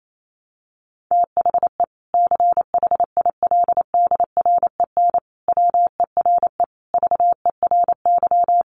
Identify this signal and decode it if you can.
Morse code, heard as a keyed tone, about 28 wpm, 700 Hz. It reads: THE CHILDREN WERE VERY
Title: Sentences from Top 1300 Words 28wpm